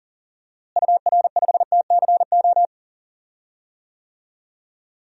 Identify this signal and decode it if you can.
Morse code, heard as a keyed tone, about 40 wpm, 700 Hz. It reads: UR5TCO
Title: DXCC-100 Call Signs - 6 Characters - Repeated 2x - 40wpm